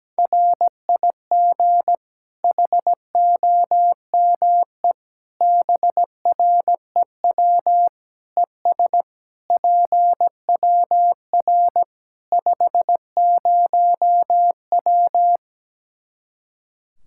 Morse code, heard as a keyed tone, about 17 wpm, 700 Hz. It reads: RIG HOME BREW ES PWR 50W
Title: K1USN SST QSO Practice 17wpm